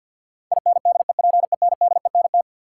Morse code, heard as a keyed tone, about 53 wpm, 700 Hz. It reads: INDEPENDENT
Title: Sentences from Top 900 Words - New Word Drill - 50wpm